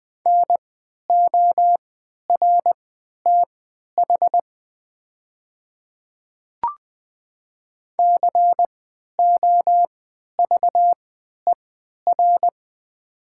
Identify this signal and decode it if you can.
Morse code, heard as a keyed tone, about 20 wpm, spaced as for 11 wpm, 700 Hz. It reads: NORTH COVER
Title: 3x Character Spacing - Top 300 Words - 20wpm